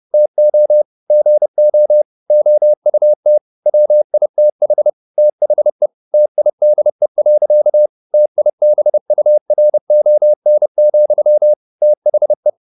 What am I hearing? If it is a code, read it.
Morse code, heard as a keyed tone, about 30 wpm, 600 Hz. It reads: TO GO OUT WITH THE TIDE. TIBURON, THE